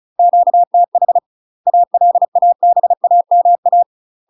Morse code, heard as a keyed tone, about 35 wpm, 700 Hz. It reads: QTH ALABAMA
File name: UK MASTER QSOs TRAINING / 35WPM / UK MASTER QSOs 0030